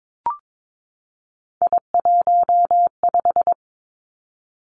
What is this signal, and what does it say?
Morse code, heard as a keyed tone, about 22 wpm, 700 Hz. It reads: I15
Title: Three Challenging Letter-Number Combinations 22wpm